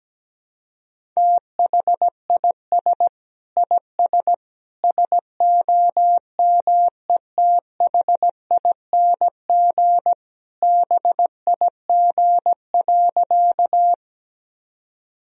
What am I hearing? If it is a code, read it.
Morse code, heard as a keyed tone, about 17 wpm, 700 Hz. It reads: THIS IS SOMETHING BIG.
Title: Sentences from Top 1000 Words - Review Part 1 - 17wpm